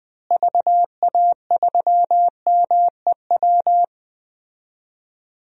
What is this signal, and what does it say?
Morse code, heard as a keyed tone, about 20 wpm, 700 Hz. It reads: VA3MEW
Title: DXCC-100 Call Signs - 6 Characters - Repeated 3x - 20wpm